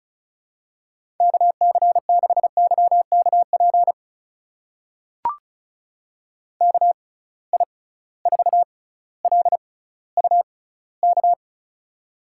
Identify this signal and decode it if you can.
Morse code, heard as a keyed tone, about 34 wpm, spaced as for 18 wpm, 700 Hz. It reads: KC6YKP KI4LUK
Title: USA Call Signs - 6 Characters - 6x Character Spacing - 35wpm